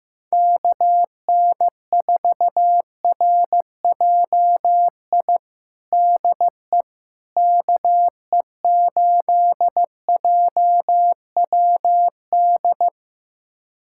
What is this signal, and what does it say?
Morse code, heard as a keyed tone, about 16 wpm, 700 Hz. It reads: KN4RJI DE KE8JWD
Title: USA Callsign DE Callsign - 6 Characters - 15wpm